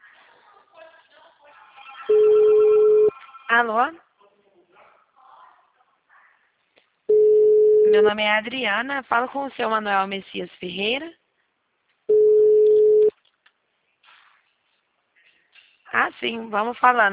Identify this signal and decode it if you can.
Ringback tone (Europe, 425 Hz), with speech.